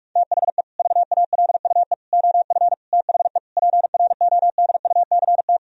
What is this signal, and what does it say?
Morse code, heard as a keyed tone, about 46 wpm, 700 Hz. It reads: THE VALUE OF THE PRODUCT
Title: Sentences from Top 1000 Words - Review Part 3 - 45wpm